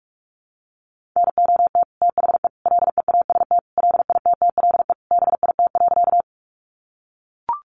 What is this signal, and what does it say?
Morse code, heard as a keyed tone, about 44 wpm, 700 Hz. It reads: NOT THE LEAST LITTLE BIT.
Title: Sentences from Top 900 Words 45wpm